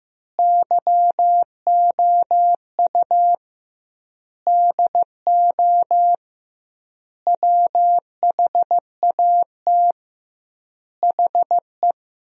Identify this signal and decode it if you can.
Morse code, heard as a keyed tone, about 15 wpm, 700 Hz. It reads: YOU DO WHAT HE
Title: 2x Word-Spacing - Sentences from Top 100 Words - 15wpm